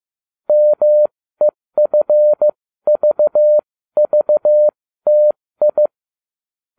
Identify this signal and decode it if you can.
Morse code, heard as a keyed tone, about 15 wpm, spaced as for 13 wpm, 600 Hz. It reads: MEFVVTI